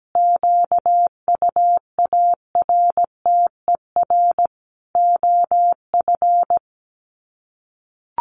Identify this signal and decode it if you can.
Morse code, heard as a keyed tone, about 17 wpm, 700 Hz. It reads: QUARTER OF
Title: Sets of 5 Words Encore2 17wpm